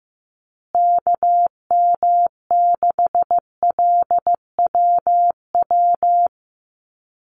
Morse code, heard as a keyed tone, about 15 wpm, 700 Hz. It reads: KM6LWW
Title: USA Call Signs Repeated 2x 15wpm